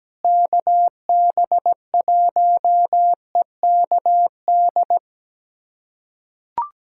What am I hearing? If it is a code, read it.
Morse code, heard as a keyed tone, about 17 wpm, 700 Hz. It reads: KB1EKD
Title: USA Callsign DE Callsign - 6 Characters - 17wpm